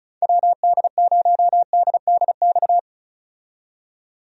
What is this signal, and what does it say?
Morse code, heard as a keyed tone, about 36 wpm, 700 Hz. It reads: WD0DDX
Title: USA Call Signs - 6 Characters - 35wpm